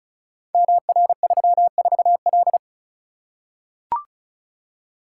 Morse code, heard as a keyed tone, about 36 wpm, 690 Hz. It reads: MR34L
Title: Code Groups - 5 Letter:Numbers - 35wpm